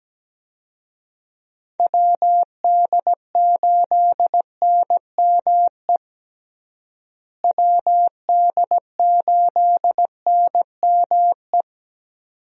Morse code, heard as a keyed tone, about 17 wpm, 700 Hz. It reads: WD8NME WD8NME
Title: USA Call Signs - 6 Characters - Repeated 2x - 17wpm